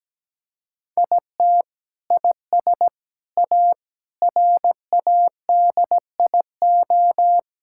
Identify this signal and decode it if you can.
Morse code, heard as a keyed tone, about 17 wpm, 700 Hz. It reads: IT IS A RADIO
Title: Sentences from Top 1000 Words - Review Part 5 - 17wpm